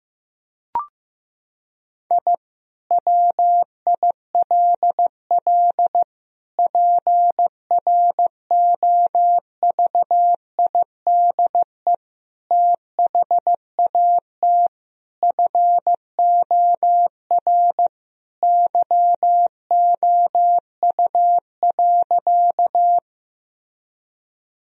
Morse code, heard as a keyed tone, about 15 wpm, 700 Hz. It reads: I WILL PROVIDE THAT FOR YOU.
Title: Sentences from Top 1000 Words - Review Part 3 - 15wpm